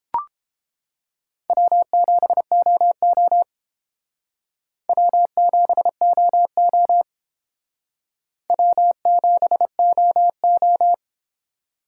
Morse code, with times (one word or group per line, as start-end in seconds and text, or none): 1.5-3.4 W7OO
4.9-7.0 W7OO
8.5-10.9 W7OO